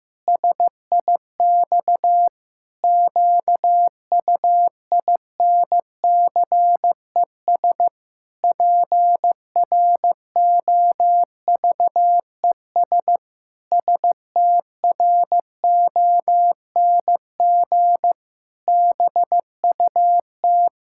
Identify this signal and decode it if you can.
Morse code, heard as a keyed tone, about 15 wpm, 700 Hz. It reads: SIX QUINCES PROVES STRONG BUT
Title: Pangram Challenge v2 - 15wpm